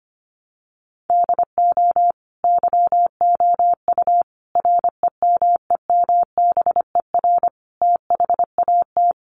Morse code, heard as a keyed tone, about 25 wpm, 700 Hz. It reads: DO YOU REMEMBER THAT
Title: Sentences from Top 1100 Words 25wpm